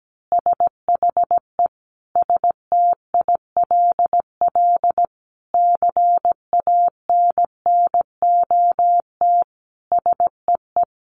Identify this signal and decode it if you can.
Morse code, heard as a keyed tone, about 17 wpm, 700 Hz. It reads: SHE STILL CANNOT SEE
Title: Additional Easy Phrases 17wpm